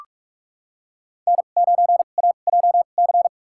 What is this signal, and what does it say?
Morse code, heard as a keyed tone, about 46 wpm, spaced as for 32 wpm, 690 Hz. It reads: N9AJC
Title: USA Call Signs - 5 Characters - 2x Character Spacing - 45wpm